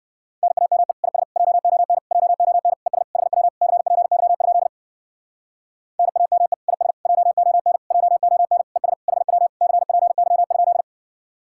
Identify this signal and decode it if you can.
Morse code, heard as a keyed tone, about 65 wpm, 700 Hz. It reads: NAME IS JON JON ES HW CPY? NAME IS JON JON ES HW CPY?